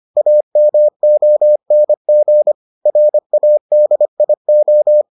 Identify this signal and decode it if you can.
Morse code, heard as a keyed tone, about 25 wpm, 600 Hz. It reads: AMONG RADIO